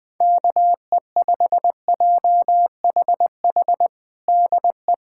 Morse code, heard as a keyed tone, about 21 wpm, 700 Hz. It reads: KE5JHH DE
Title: USA Callsign DE Callsign - 6 Characters - 20wpm